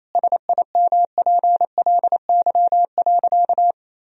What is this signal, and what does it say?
Morse code, heard as a keyed tone, about 28 wpm, 700 Hz. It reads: SIMPLY.
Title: Sentences from Top 1000 Words - Review Part 4 - 28wpm